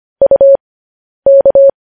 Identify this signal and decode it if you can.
Morse code, heard as a keyed tone, about 25 wpm, spaced as for 10 wpm, 550 Hz. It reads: UK
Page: news headlines in morse code at 10 wpm